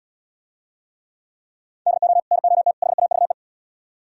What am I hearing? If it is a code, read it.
Morse code, heard as a keyed tone, about 77 wpm, 700 Hz. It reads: DO NOT HAVE